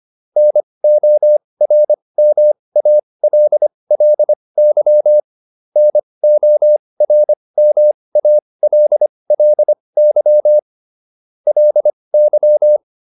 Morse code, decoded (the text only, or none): NORMALLY NORMALLY LY